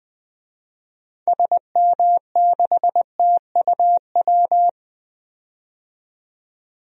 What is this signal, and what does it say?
Morse code, heard as a keyed tone, about 20 wpm, 700 Hz. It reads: SM6TUW
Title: DXCC-100 Call Signs - 6 Characters - 20wpm